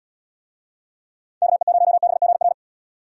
Morse code, heard as a keyed tone, about 71 wpm, 700 Hz. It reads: KE0BGR